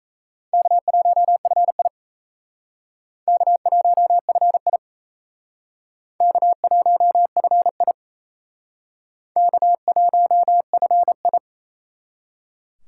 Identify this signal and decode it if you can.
Morse code, heard as a keyed tone, about 34 wpm, 700 Hz. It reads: K1FI K1FI K1FI K1FI
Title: Speed Racing - USA 1x2 Call Signs 28wpm